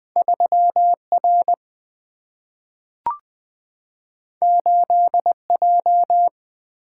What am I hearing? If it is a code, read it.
Morse code, heard as a keyed tone, about 20 wpm, 700 Hz. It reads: 3 R 8 J
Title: Single-Letter Single-Number 20wpm